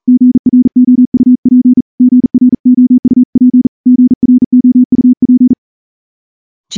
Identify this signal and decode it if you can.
Morse code, heard as a keyed tone, about 40 wpm, 250 Hz. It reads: GROUP GROUP GROUP
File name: morse-training-40WPM